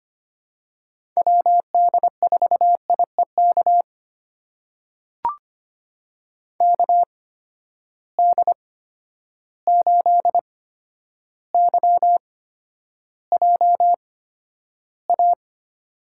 Morse code, heard as a keyed tone, about 25 wpm, spaced as for 11 wpm, 700 Hz. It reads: WD4IEK KD8YJA
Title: USA Call Signs - 6 Characters - 8x Character Spacing - 25wpm